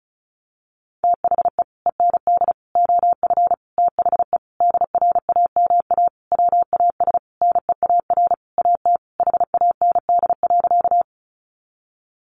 Morse code, heard as a keyed tone, about 34 wpm, 700 Hz. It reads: THE END OF THE DRAMA WAS NEAR AT HAND.
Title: Sentences from Top 1500 Words 35wpm